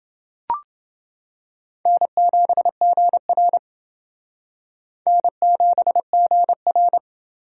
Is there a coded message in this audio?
Yes: Morse code from 1.9 to 7.0 s, heard as a keyed tone, about 28 wpm, 700 Hz. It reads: N7GR N7GR